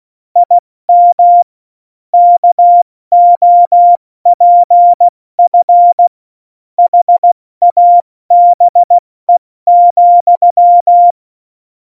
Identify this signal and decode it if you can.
Morse code, heard as a keyed tone, about 16 wpm, spaced as for 14 wpm, 700 Hz. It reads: IM KOPF HABE,
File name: Morsekurs / Witze / Witz 20. Februar 2018, 70 F80